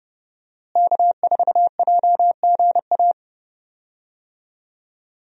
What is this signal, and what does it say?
Morse code, heard as a keyed tone, about 30 wpm, 700 Hz. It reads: K4JGA
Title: USA Call Signs - 5 Characters - Repeated 3x - 30wpm